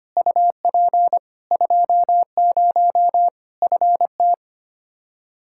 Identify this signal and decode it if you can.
Morse code, heard as a keyed tone, about 25 wpm, 700 Hz. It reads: UP 20 FT
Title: Practice QSO - Exchange 2 - Info - 25wpm